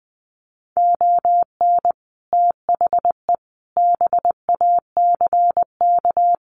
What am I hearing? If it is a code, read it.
Morse code, heard as a keyed tone, about 20 wpm, 700 Hz. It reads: ON THE BACK
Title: Sets of 4 Words 20wpm